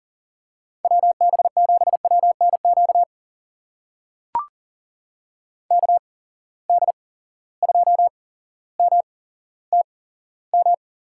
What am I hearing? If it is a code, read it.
Morse code, heard as a keyed tone, about 40 wpm, spaced as for 17 wpm, 700 Hz. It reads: WB7WNQ KD2MTM
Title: USA Call Signs - 6 Characters - 8x Character Spacing - 40wpm